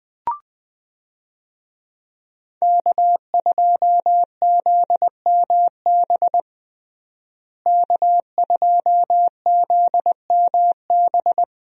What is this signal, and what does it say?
Morse code, heard as a keyed tone, about 20 wpm, 700 Hz. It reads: K2ZMB K2ZMB
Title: USA Call Signs - 5 Characters - Repeated 3x - 20wpm